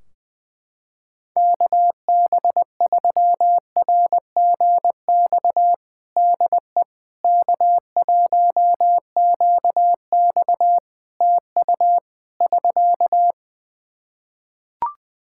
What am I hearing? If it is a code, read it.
Morse code, heard as a keyed tone, about 21 wpm, 700 Hz. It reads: KB3RGX DE K1QX TU <SK>
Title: Practice QSO - Exchange 4 - Ending - 20wpm